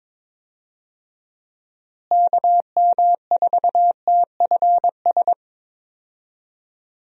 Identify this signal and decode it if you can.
Morse code, heard as a keyed tone, about 21 wpm, 700 Hz. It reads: KM4TFS
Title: USA Call Signs - 6 Characters - Repeated 3x - 22wpm